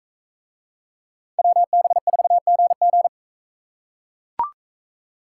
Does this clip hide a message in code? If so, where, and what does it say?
Yes: Morse code from 1.4 to 3.1 s, heard as a keyed tone, about 41 wpm, 700 Hz. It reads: WB4GG